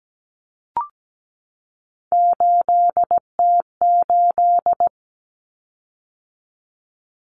Morse code, heard as a keyed tone, about 17 wpm, 700 Hz. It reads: 8T8
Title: Common Cut Numbers - 3-Digit 17wpm